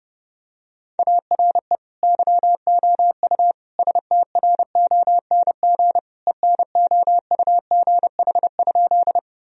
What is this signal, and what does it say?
Morse code, heard as a keyed tone, about 30 wpm, 700 Hz. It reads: ARE YOU STRONG ENOUGH?
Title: Sentences from Top 1000 Words - Review Part 3 - 30wpm